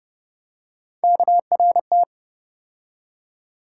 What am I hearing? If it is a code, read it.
Morse code, heard as a keyed tone, about 36 wpm, 700 Hz. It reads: KRT